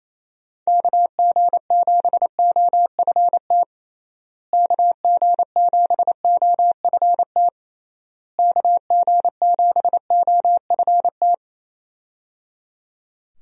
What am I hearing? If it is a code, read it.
Morse code, heard as a keyed tone, about 28 wpm, 700 Hz. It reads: KG7OFT KG7OFT KG7OFT
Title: USA Call Signs - 6 Characters - Repeated 3x - 28wpm